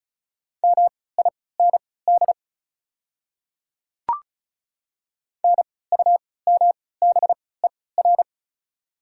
Morse code, heard as a keyed tone, about 34 wpm, spaced as for 20 wpm, 690 Hz. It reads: MIND NUMBER
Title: 3x Character Spacing - Top 500 Words - 35wpm